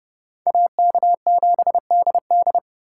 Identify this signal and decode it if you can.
Morse code, heard as a keyed tone, about 37 wpm, 700 Hz. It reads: AK7DD
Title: Speed Racing - USA 2x2 Call Signs 30wpm